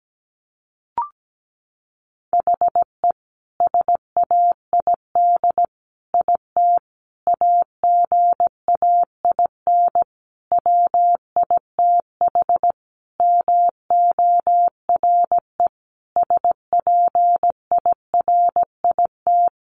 Morse code, heard as a keyed tone, about 17 wpm, 700 Hz. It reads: HE SAID IT AGAIN WITH MORE SPIRIT
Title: Sentences from Top 1400 Words 17wpm